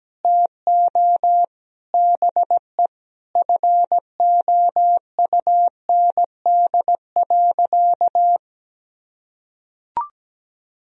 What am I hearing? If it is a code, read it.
Morse code, heard as a keyed tone, about 17 wpm, 700 Hz. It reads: TO BE FOUND.
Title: Sentences from Top 1000 Words - Review Part 3 - 17wpm